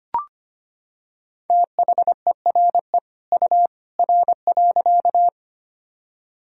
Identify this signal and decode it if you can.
Morse code, heard as a keyed tone, about 25 wpm, 700 Hz. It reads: THERE U R.
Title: Sentences from Top 100 Words and Abbreviations 25wpm